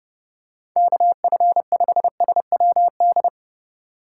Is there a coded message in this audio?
Yes: Morse code from 0.8 to 3.3 s, heard as a keyed tone, about 28 wpm, 700 Hz. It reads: KF5SWD